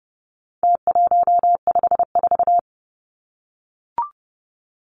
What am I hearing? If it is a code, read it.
Morse code, heard as a keyed tone, about 30 wpm, 700 Hz. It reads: T154
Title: Numbers - Contest Style - 4-Digit 30wpm